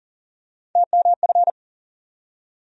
Morse code, heard as a keyed tone, about 37 wpm, 700 Hz. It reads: TMF
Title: USA Call Sign Postfixes - 3 Letters 40wpm